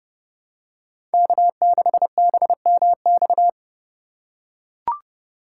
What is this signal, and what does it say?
Morse code, heard as a keyed tone, about 30 wpm, 700 Hz. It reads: K6BMX
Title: USA Call Signs - 2x Character Spacing - 30wpm